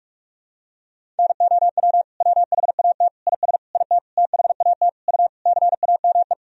Morse code, heard as a keyed tone, about 46 wpm, 700 Hz. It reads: NOW WHAT IS IT THAT U CAME
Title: Sentences from Top 200 Words and Abbreviations 45wpm